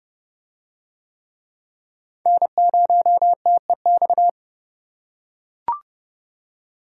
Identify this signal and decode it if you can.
Morse code, heard as a keyed tone, about 27 wpm, 700 Hz. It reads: N0TEX